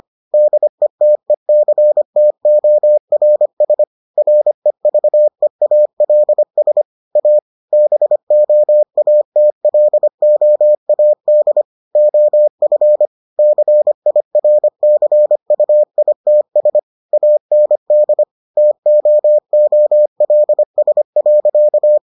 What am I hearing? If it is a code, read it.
Morse code, heard as a keyed tone, about 25 wpm, 600 Hz. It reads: DETECTORS REVEALS A BOATLOAD OF CIRCUITS AND TOOLS.